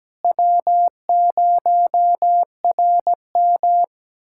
Morse code, heard as a keyed tone, about 17 wpm, 700 Hz. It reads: W0RM
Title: USA Callsign DE Callsign - 4 Characters - 17wpm